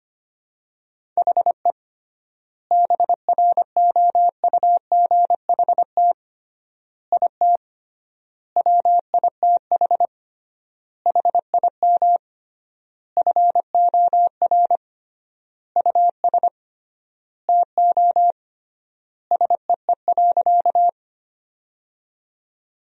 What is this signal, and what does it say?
Morse code, heard as a keyed tone, about 25 wpm, 700 Hz. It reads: HE BROUGHT IT WITH HIM FOR US TO SEE.
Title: 3x Word-Spacing - Sentences from Top 500 Words - 25wpm